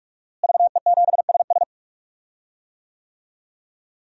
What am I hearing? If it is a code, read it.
Morse code, heard as a keyed tone, about 44 wpm, 700 Hz. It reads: VE7SS
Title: DXCC-100 Call Signs - 5 Characters - Repeated 3x - 45wpm